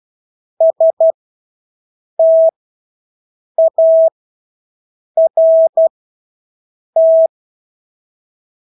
Morse code, heard as a keyed tone, about 12 wpm, spaced as for 6 wpm, 650 Hz. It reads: START
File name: Engelske ord 12-6 wpm